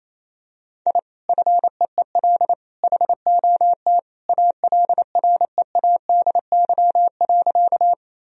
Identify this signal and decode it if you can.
Morse code, heard as a keyed tone, about 27 wpm, 700 Hz. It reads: I FEEL HOT ALREADY.
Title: Sentences from Top 1000 Words - Review Part 3 - 28wpm